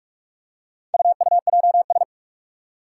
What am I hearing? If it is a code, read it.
Morse code, heard as a keyed tone, about 44 wpm, 700 Hz. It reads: UUJS
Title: Four Challenging Letter-Number Combinations 45wpm